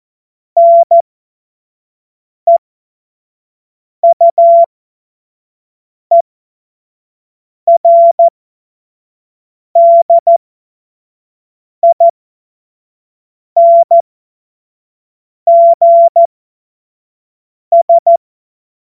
Morse code, heard as a keyed tone, about 14 wpm, spaced as for 5 wpm, 680 Hz. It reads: NEUERDINGS
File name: Morsekurs / Witze / Witz 07. Dezember 2021, 25 F70